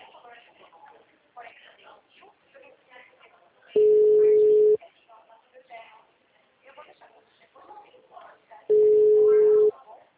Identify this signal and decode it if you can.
Ringback tone (Europe, 425 Hz), with speech.
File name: out- -2008- - - .487